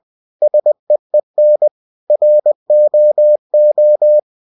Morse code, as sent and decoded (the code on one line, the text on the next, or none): ... . -. / .-. --- ---
SEEN ROO